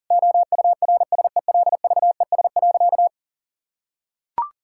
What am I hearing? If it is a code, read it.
Morse code, heard as a keyed tone, about 40 wpm, 700 Hz. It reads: OURSELVES.
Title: Sentences from Top 1600 Words 40wpm